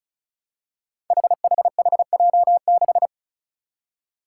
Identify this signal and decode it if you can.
Morse code, heard as a keyed tone, about 36 wpm, 700 Hz. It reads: HHHJ6
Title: Five Challenging Letter-Number Combinations 35wpm